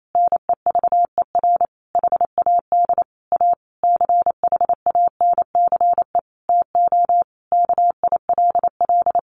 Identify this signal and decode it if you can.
Morse code, heard as a keyed tone, about 28 wpm, 700 Hz. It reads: NEVER HAD A CHANCE TO KILL
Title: Sentences from Top 1400 Words 28wpm